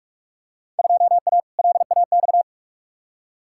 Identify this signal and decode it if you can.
Morse code, heard as a keyed tone, about 42 wpm, 700 Hz. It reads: 2A LAX
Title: Field Day Exchanges - 45wpm